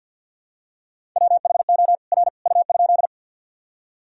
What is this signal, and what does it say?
Morse code, heard as a keyed tone, about 47 wpm, 700 Hz. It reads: WHO R U?
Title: Sentences from Top 100 Words and Abbreviations 50wpm